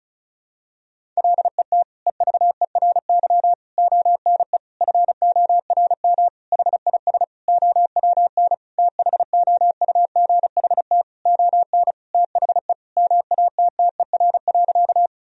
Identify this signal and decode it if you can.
Morse code, heard as a keyed tone, about 36 wpm, 700 Hz. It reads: LET EVERY ONE FORM HIS OWN THOUGHT ON THE MATTER.
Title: Sentences from Top 1000 Words - Review Part 1 - 35wpm